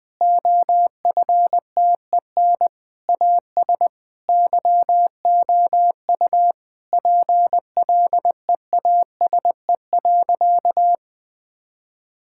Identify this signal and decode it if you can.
Morse code, heard as a keyed tone, about 20 wpm, 700 Hz. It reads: OFTEN AS YOU PLEASE.
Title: Sentences from Top 1600 Words 20wpm